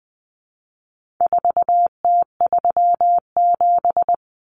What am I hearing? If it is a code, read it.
Morse code, heard as a keyed tone, about 20 wpm, 700 Hz. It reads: 4T37
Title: Common Cut Numbers - 4-Digit 20wpm